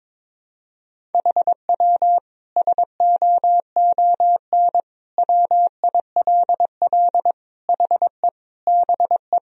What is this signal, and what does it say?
Morse code, heard as a keyed tone, about 22 wpm, 700 Hz. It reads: HW SOON WILL HE BE
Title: Sentences from Top 400 Words and Abbreviations 22wpm